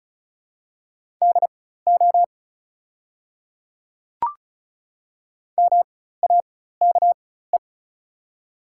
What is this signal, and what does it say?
Morse code, heard as a keyed tone, about 36 wpm, spaced as for 16 wpm, 700 Hz. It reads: DO MAKE
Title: 4x Character Spacing - Top 100 Words - 35wpm